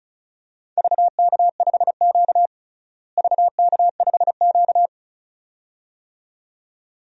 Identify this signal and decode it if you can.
Morse code, heard as a keyed tone, about 36 wpm, 700 Hz. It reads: VK5Q VK5Q
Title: DXCC-100 Call Signs - 4 Characters - Repeated 2x - 35wpm